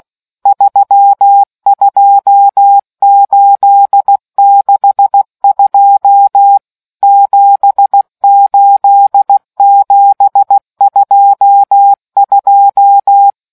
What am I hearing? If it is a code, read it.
Morse code, heard as a keyed tone, about 16 wpm, 800 Hz. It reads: 32862 78722